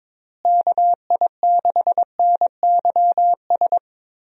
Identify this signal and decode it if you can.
Morse code, heard as a keyed tone, about 22 wpm, 700 Hz. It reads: KI6NYS
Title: USA Call Signs - 6 Characters - Repeated 3x - 22wpm